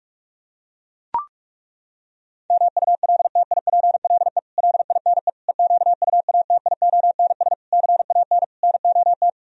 Morse code, heard as a keyed tone, about 37 wpm, 700 Hz. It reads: MULTIPLE LINE EQUATIONS CAN NOT
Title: Speed-Racing - Sets of 5 Words 30wpm